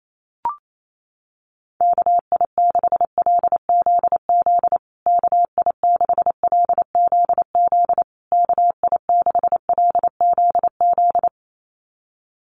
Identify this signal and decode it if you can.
Morse code, heard as a keyed tone, about 28 wpm, 700 Hz. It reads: KI6LZZ KI6LZZ KI6LZZ
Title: USA Call Signs Repeated 3x 28wpm